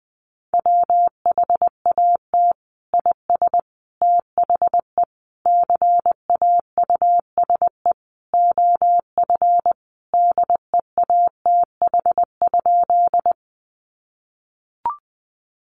Morse code, heard as a keyed tone, about 20 wpm, 700 Hz. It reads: WHAT IS THE CAUSE OF DEATH?
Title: Sentences from Top 1100 Words 20wpm